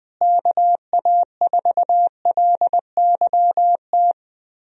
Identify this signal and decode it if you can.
Morse code, heard as a keyed tone, about 20 wpm, 700 Hz. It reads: KA4LYT
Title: USA Call Signs - 6 Characters - Repeated 3x - 20wpm